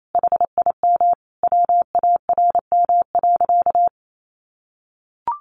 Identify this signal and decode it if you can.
Morse code, heard as a keyed tone, about 28 wpm, 700 Hz. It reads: HIM WARM.
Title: Sentences from Top 1500 Words 28wpm